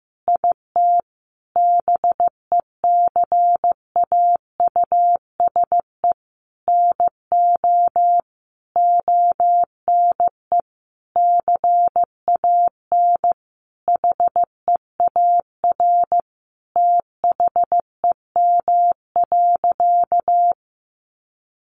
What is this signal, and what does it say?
Morse code, heard as a keyed tone, about 15 wpm, 700 Hz. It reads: IT BECAUSE NO ONE CAN HEAR THEM.
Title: Sentences from Top 600 Words 15wpm